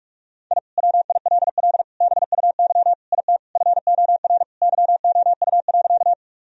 Morse code, heard as a keyed tone, about 46 wpm, 700 Hz. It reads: I WILL BUY IT FOR YOU.
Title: Sentences from Top 1000 Words - Review Part 4 - 45wpm